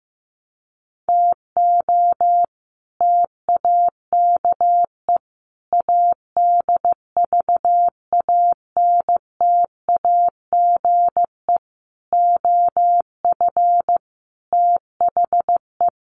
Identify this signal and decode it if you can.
Morse code, heard as a keyed tone, about 15 wpm, 700 Hz. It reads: TO TAKE ADVANTAGE OF THE
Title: Sets of 5 Words 15wpm